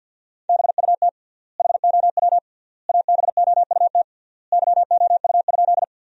Morse code, heard as a keyed tone, about 50 wpm, 700 Hz. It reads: BUT HOW ABOUT YOU?
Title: 3x Word-Spacing - Sentences from Top 100 Words - 50wpm